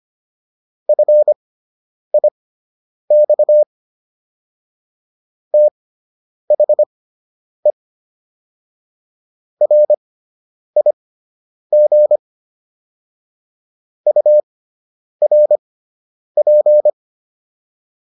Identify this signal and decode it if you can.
Morse code, heard as a keyed tone, about 25 wpm, spaced as for 9 wpm, 600 Hz. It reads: FIX THE RIG URP